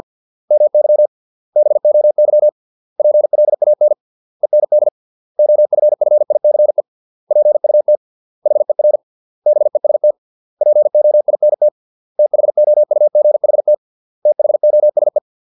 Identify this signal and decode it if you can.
Morse code, heard as a keyed tone, about 50 wpm, 600 Hz. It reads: MY BOY PLAN END OFFICE PUT HER BEST POINT THOUGHT THOSE